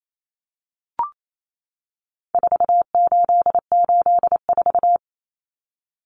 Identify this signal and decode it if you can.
Morse code, heard as a keyed tone, about 28 wpm, 700 Hz. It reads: 4884
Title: Common Cut Numbers - 4-Digit 28wpm